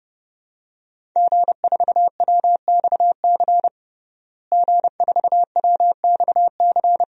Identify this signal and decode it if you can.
Morse code, heard as a keyed tone, about 30 wpm, 700 Hz. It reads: G4WXC G4WXC
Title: DXCC-100 Call Signs - 5 Characters - Repeated 2x - 30wpm